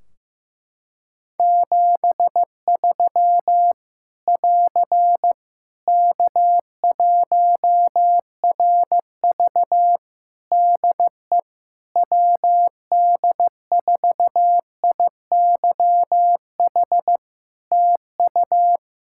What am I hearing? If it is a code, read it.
Morse code, heard as a keyed tone, about 15 wpm, 700 Hz. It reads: 73 <AR> K1RV DE WD4IYH TU
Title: Practice QSO - Exchange 4 - Ending - 15wpm